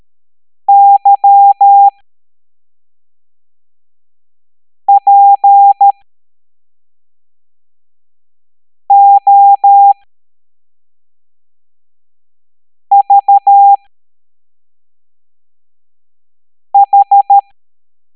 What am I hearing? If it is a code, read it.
Morse code, heard as a keyed tone, about 13 wpm, spaced as for 3 wpm, 800 Hz. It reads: YPOVH